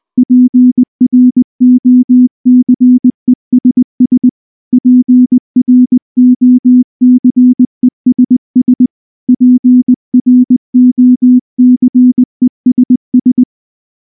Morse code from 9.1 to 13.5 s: PROCESS